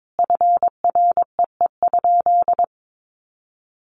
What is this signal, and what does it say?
Morse code, heard as a keyed tone, about 22 wpm, 700 Hz. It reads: FREE?
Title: Sentences from Top 500 Words 22wpm